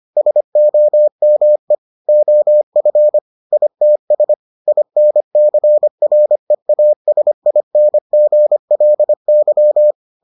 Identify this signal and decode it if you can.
Morse code, heard as a keyed tone, about 25 wpm, 600 Hz. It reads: SOME OF ITS INCREASINGLY